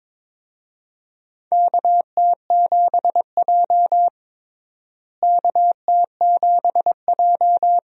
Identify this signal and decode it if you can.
Morse code, heard as a keyed tone, about 22 wpm, 700 Hz. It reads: KT7J KT7J
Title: USA Call Signs - 4 Characters - Repeated 2x - 22wpm